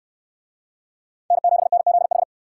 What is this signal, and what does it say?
Morse code, heard as a keyed tone, about 69 wpm, 700 Hz. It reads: N7NZH